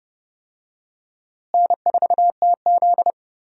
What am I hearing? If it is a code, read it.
Morse code, heard as a keyed tone, about 30 wpm, 700 Hz. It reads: N4TZ
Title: USA Call Signs - 4 Characters - Repeated 2x - 30wpm